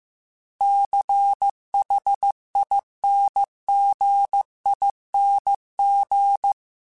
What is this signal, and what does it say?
Morse code, heard as a keyed tone, about 15 wpm, 800 Hz. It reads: CHINGING